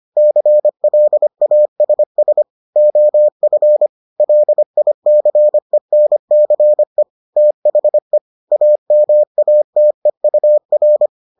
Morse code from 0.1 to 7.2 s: CLASS OF LICENCE

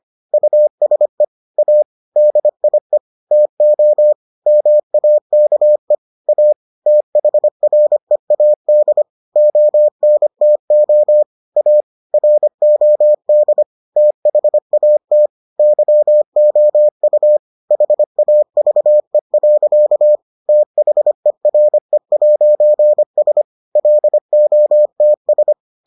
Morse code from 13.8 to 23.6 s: THAT YOU HAVE. THERE'S

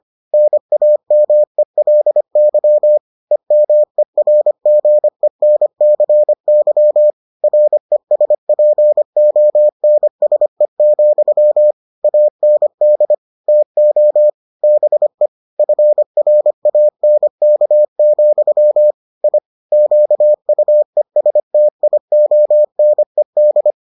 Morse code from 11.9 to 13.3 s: AND

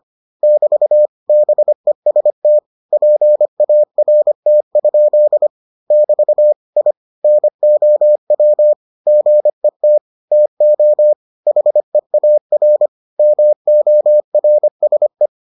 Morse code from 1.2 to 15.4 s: BEST PART? <BT> I NOW GET TO HEAR MORSE

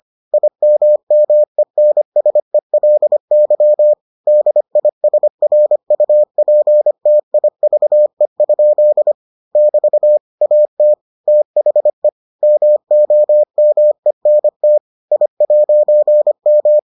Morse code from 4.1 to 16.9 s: DISRUPTIVE? <BT> AT THE MOMENT I'M

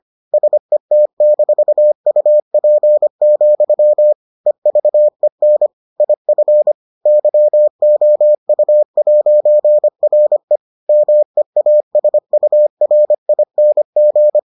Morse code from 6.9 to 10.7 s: YOU'RE

